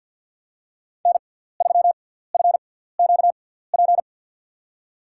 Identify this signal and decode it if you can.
Morse code, heard as a keyed tone, about 52 wpm, spaced as for 17 wpm, 700 Hz. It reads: N3FQP
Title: USA Call Signs - 5 Characters - 6x Character Spacing - 50wpm